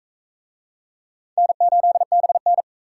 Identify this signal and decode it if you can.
Morse code, heard as a keyed tone, about 38 wpm, 700 Hz. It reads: N8BN